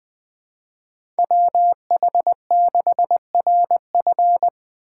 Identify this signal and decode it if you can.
Morse code, heard as a keyed tone, about 20 wpm, 700 Hz. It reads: WH6RF